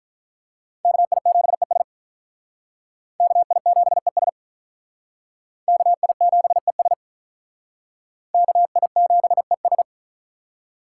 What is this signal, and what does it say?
Morse code, heard as a keyed tone, about 48 wpm, 700 Hz. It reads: KI7ES KI7ES KI7ES KI7ES